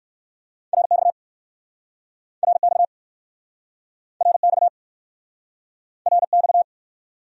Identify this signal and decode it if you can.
Morse code, heard as a keyed tone, about 61 wpm, 700 Hz. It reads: RX RX RX RX